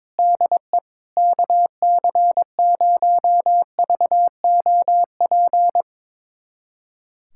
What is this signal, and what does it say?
Morse code, heard as a keyed tone, about 22 wpm, 700 Hz. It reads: DE KC0VOP
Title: USA Callsign DE Callsign 22wpm